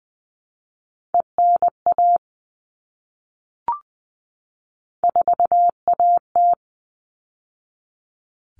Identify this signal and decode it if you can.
Morse code, heard as a keyed tone, about 20 wpm, 700 Hz. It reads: ENA 4AT
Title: All Cut Numbers - 3-Digit 20wpm